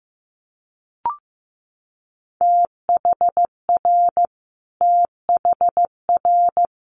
Morse code, heard as a keyed tone, about 15 wpm, 700 Hz. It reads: THR THR
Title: Multi-letter Phonemes Repeated 3x 15wpm